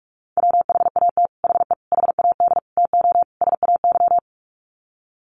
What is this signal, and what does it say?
Morse code, heard as a keyed tone, about 44 wpm, 700 Hz. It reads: WHAT HE HAD TO SAY
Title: Sets of 5 Words Encore2 45wpm